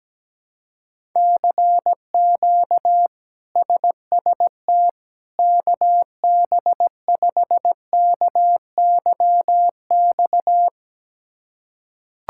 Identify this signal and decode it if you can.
Morse code, heard as a keyed tone, about 17 wpm, 700 Hz. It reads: CQ SST KB5KYX
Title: K1USN SST QSO Practice - 17wpm